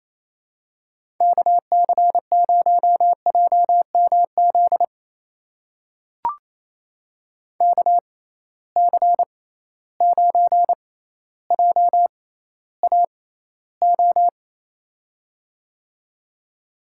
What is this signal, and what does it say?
Morse code, heard as a keyed tone, about 28 wpm, spaced as for 14 wpm, 700 Hz. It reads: KC0JMZ KC9JAO
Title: USA Call Signs - 6 Characters - 6x Character Spacing - 28wpm